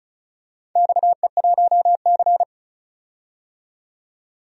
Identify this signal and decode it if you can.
Morse code, heard as a keyed tone, about 36 wpm, 700 Hz. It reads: XE1C
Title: DXCC-100 Call Signs - 4 Characters - 35wpm